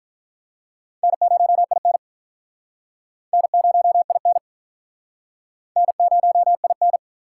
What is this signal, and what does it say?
Morse code, heard as a keyed tone, about 47 wpm, 700 Hz. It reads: N0IN N0IN N0IN